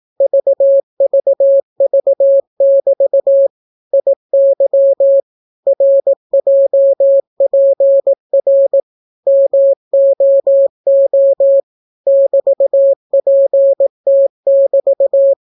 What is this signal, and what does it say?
Morse code, heard as a keyed tone, about 18 wpm, 550 Hz. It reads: VVV<BT> IY RJPR MOO <BT>PT<BT>